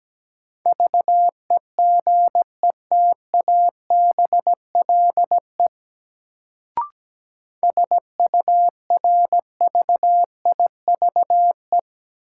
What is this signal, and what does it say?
Morse code, heard as a keyed tone, about 17 wpm, 700 Hz. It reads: VEGETABLE SURVIVE
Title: Sentences from Top 1300 Words - New Word Drill Alt - 17wpm